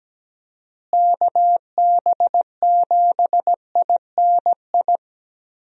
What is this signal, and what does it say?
Morse code, heard as a keyed tone, about 17 wpm, 700 Hz. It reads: KB7INI
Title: USA Call Signs - 6 Characters - Repeated 2x - 17wpm